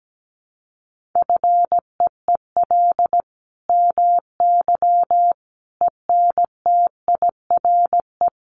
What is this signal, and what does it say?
Morse code, heard as a keyed tone, about 17 wpm, 700 Hz. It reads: FEEL MY ENTIRE
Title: Sentences from Top 800 Words 17wpm